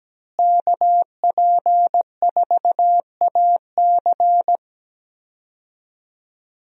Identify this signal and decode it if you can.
Morse code, heard as a keyed tone, about 17 wpm, 700 Hz. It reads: KP4AC
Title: USA Call Signs - 5 Characters - Repeated 3x - 17wpm